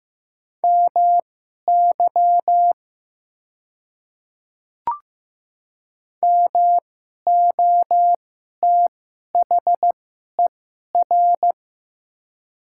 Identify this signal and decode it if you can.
Morse code, heard as a keyed tone, about 15 wpm, spaced as for 11 wpm, 700 Hz. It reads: MY MOTHER
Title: 2x Character Spacing - Top 200 Words - 15wpm